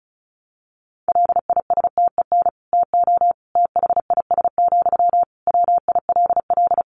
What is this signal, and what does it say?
Morse code, heard as a keyed tone, about 34 wpm, 700 Hz. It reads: LISTEN TO THIS, WILL
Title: Sentences from Top 400 Words 35wpm